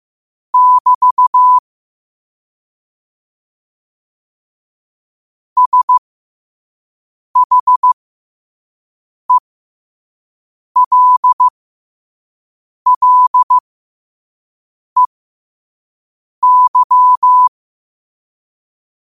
Morse code, heard as a keyed tone, about 15 wpm, spaced as for 5 wpm, 1000 Hz. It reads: <BT> SHELLEY